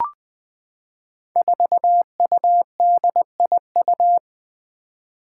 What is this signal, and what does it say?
Morse code, heard as a keyed tone, about 20 wpm, 700 Hz. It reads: 4UDIU
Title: Five Challenging Letter-Number Combinations 20wpm